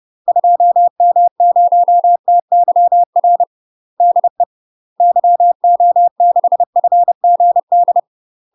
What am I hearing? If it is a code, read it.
Morse code, heard as a keyed tone, about 30 wpm, 700 Hz. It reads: 2M0TYR DE YO6FGD
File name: UK MASTER QSOs TRAINING / 30WPM / UK MASTER QSOs 0049